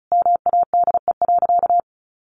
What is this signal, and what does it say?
Morse code, heard as a keyed tone, about 34 wpm, 700 Hz. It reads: MADE.
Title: Sentences from Top 1500 Words 35wpm